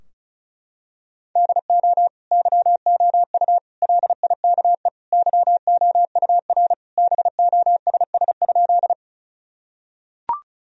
Morse code, heard as a keyed tone, about 34 wpm, 700 Hz. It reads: DO YOU LIKE YOUR BOSS?
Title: Sentences from Top 1000 Words - Review Part 4 - 35wpm